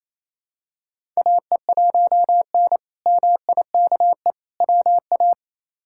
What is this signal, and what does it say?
Morse code, heard as a keyed tone, about 28 wpm, 700 Hz. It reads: AE1N MIKE WA
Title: K1USN SST QSO Practice - 28wpm